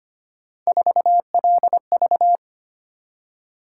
Morse code, heard as a keyed tone, about 25 wpm, 700 Hz. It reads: 4LV
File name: Three Letter-Number Combinations - 25wpm